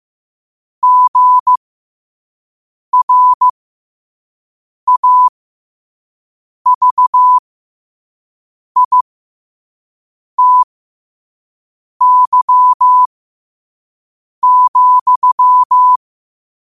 Morse code, heard as a keyed tone, about 15 wpm, spaced as for 5 wpm, 1000 Hz. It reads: GRAVITY,
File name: qotdfiles / 5wpm